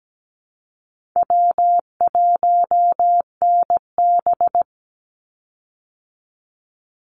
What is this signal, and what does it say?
Morse code, heard as a keyed tone, about 17 wpm, 700 Hz. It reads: W1NB
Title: USA Call Signs 17wpm